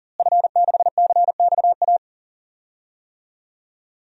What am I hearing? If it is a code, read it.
Morse code, heard as a keyed tone, about 40 wpm, 700 Hz. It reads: F6CXA
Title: DXCC-100 Call Signs - 5 Characters - Repeated 3x - 40wpm